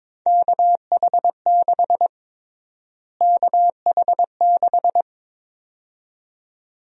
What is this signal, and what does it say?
Morse code, heard as a keyed tone, about 22 wpm, 700 Hz. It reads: KH6 KH6
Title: USA Call Sign Prefixes - Repeated 2x - 22wpm